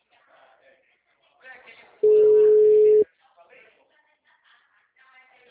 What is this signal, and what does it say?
Ringback tone (Europe, 425 Hz).